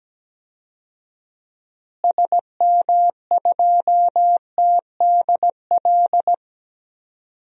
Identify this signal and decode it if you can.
Morse code, heard as a keyed tone, about 17 wpm, 700 Hz. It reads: SM2TDL
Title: DXCC-100 Call Signs - 6 Characters - Repeated 3x - 17wpm